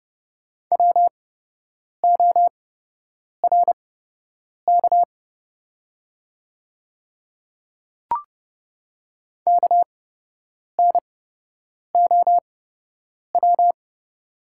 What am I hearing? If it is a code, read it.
Morse code, heard as a keyed tone, about 30 wpm, spaced as for 8 wpm, 700 Hz. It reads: WORK KNOW
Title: 8x Character Spacing - Top 300 Words - 30wpm